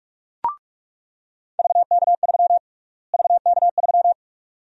Morse code, heard as a keyed tone, about 46 wpm, 700 Hz. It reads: VK3 VK3
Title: USA Call Sign Prefixes - Repeated 2x - 45wpm